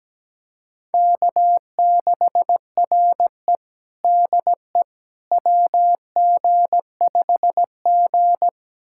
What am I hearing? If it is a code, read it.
Morse code, heard as a keyed tone, about 17 wpm, 700 Hz. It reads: K6RE DE WG5G
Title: USA Callsign DE Callsign - 4 Characters - 17wpm